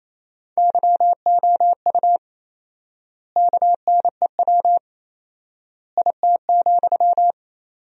Morse code, heard as a keyed tone, about 28 wpm, 700 Hz. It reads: YOU KNEW IT,
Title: 4x Word-Spacing - Sentences from Top 500 Words - 28wpm